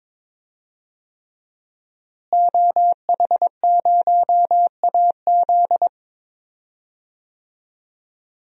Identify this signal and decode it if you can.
Morse code, heard as a keyed tone, about 22 wpm, 700 Hz. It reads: OH0AZ
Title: DXCC-100 Call Signs - 5 Characters - Repeated 3x - 22wpm